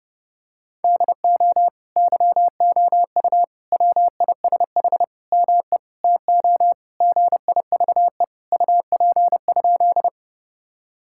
Morse code, heard as a keyed tone, about 30 wpm, 700 Hz. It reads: DO YOU WISH ME TO GIVE UP?
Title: Sentences from Top 1600 Words 30wpm